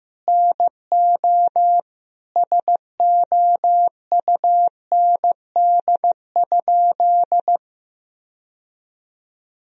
Morse code, heard as a keyed tone, about 15 wpm, 700 Hz. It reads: NO SOUND?
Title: Sentences from Top 1000 Words - Review Part 1 - 15wpm